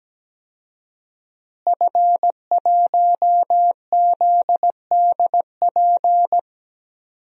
Morse code, heard as a keyed tone, about 17 wpm, 700 Hz. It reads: F1ZDP
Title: DXCC-100 Call Signs - 5 Characters - Repeated 3x - 17wpm